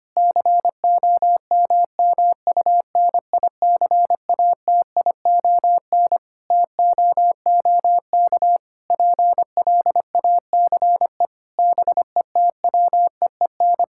Morse code, heard as a keyed tone, about 25 wpm, 700 Hz. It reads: COMMUNICATION TOOK PLACE BETWEEN
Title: Sentences from Top 1000 Words - Review Part 6 - 25wpm